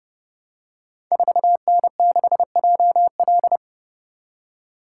Morse code, heard as a keyed tone, about 30 wpm, 700 Hz. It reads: 4N6JL
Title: Code Groups - 5 Letter:Numbers - 30wpm